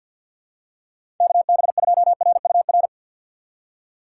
Morse code, heard as a keyed tone, about 53 wpm, 700 Hz. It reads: KB2RUR
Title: USA Call Signs - 6 Characters - 6x Character Spacing - 50wpm